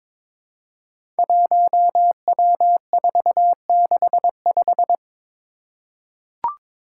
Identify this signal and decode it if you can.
Morse code, heard as a keyed tone, about 21 wpm, 700 Hz. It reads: 1W465
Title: Five Challenging Letter-Number Combinations 22wpm